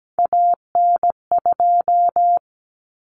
Morse code, heard as a keyed tone, about 17 wpm, 700 Hz. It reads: AN2
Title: Common Cut Numbers - 3-Digit 17wpm